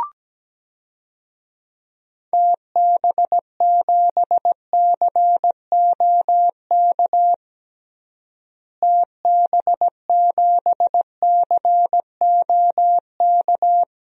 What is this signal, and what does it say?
Morse code, heard as a keyed tone, about 17 wpm, 700 Hz. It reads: TB7COK TB7COK
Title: DXCC-100 Call Signs - 6 Characters - Repeated 3x - 17wpm